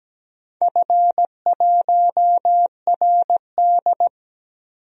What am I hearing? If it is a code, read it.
Morse code, heard as a keyed tone, about 17 wpm, 700 Hz. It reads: F1RD
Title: DXCC-100 Call Signs - 4 Characters - Repeated 2x - 17wpm